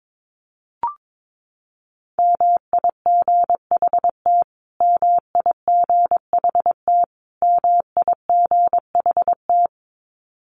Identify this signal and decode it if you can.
Morse code, heard as a keyed tone, about 22 wpm, 700 Hz. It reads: MIGHT MIGHT MIGHT
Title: Top 300 Words Repeated 22wpm